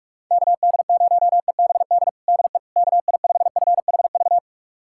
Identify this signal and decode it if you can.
Morse code, heard as a keyed tone, about 45 wpm, 700 Hz. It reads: KD0EBD DE KI5FHV
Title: USA Callsign DE Callsign - 6 Characters - 45wpm